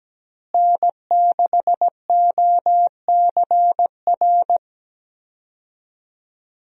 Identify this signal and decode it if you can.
Morse code, heard as a keyed tone, about 17 wpm, 700 Hz. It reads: N6OCR
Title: USA Call Signs - 5 Characters - Repeated 2x - 17wpm